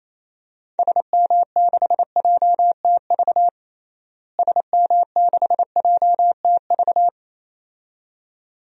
Morse code, heard as a keyed tone, about 28 wpm, 700 Hz. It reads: SM6JTV SM6JTV
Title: DXCC-100 Call Signs - 6 Characters - Repeated 2x - 28wpm